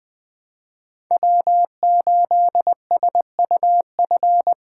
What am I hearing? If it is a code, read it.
Morse code, heard as a keyed tone, about 20 wpm, 700 Hz. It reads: W8SUF
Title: USA Call Signs - 5 Characters - 4x Character Spacing - 20wpm